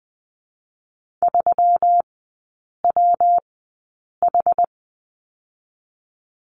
Morse code, heard as a keyed tone, about 19 wpm, spaced as for 8 wpm, 700 Hz. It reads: 3WH
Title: 152 Lesson 25e - Letter p - Three Characters 20wpm